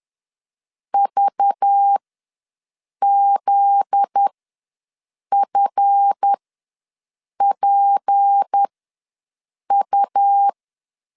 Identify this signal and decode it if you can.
Morse code, heard as a keyed tone, about 10 wpm, spaced as for 6 wpm, 800 Hz. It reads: VZFPU
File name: LEZIONI / 9 286 ws 10wpm 4032